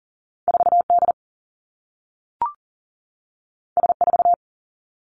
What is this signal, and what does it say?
Morse code, heard as a keyed tone, about 39 wpm, 700 Hz. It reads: 4D S4
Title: Two Challenging Letter-Number Combinations 40wpm